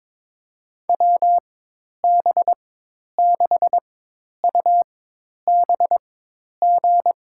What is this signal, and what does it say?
Morse code, heard as a keyed tone, about 22 wpm, spaced as for 10 wpm, 700 Hz. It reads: WB6UBG
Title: USA Call Signs - 6 Characters - 4x Character Spacing - 22wpm